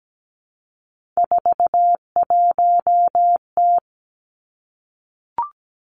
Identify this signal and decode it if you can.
Morse code, heard as a keyed tone, about 17 wpm, 700 Hz. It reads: 41T
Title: Three Challenging Letter-Number Combinations 17wpm